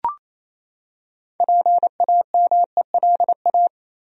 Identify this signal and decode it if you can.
Morse code, heard as a keyed tone, about 27 wpm, 700 Hz. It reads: PAMELA
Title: USA Common Names - 28wpm